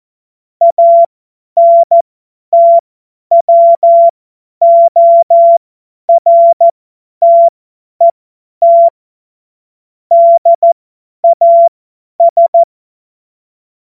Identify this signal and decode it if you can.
Morse code, heard as a keyed tone, about 14 wpm, spaced as for 10 wpm, 680 Hz. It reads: ANTWORTET DAS
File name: Morsekurs / Witze / Witz 26. Oktober 2021, 50 F70